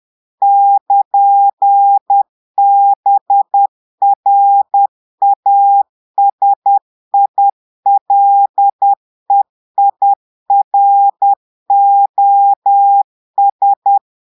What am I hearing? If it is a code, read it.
Morse code, heard as a keyed tone, about 10 wpm, 800 Hz. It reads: <KN>BRASILEIROS
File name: Portugais - texte en clair 1 - vitesse - 10 mots minute